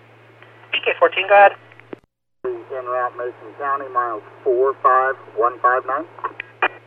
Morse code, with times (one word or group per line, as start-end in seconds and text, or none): none